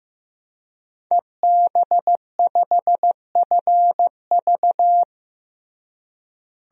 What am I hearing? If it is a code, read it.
Morse code, heard as a keyed tone, about 16 wpm, 700 Hz. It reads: EB5FV
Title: DXCC-100 Call Signs - 5 Characters - Repeated 3x - 15wpm